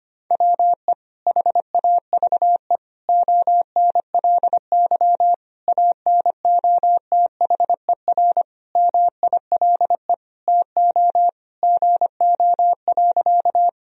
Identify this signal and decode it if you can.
Morse code, heard as a keyed tone, about 25 wpm, 700 Hz. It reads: WE HAVE ONLY ANOTHER MILE TO GO.
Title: Sentences from Top 1000 Words - Review Part 1 - 25wpm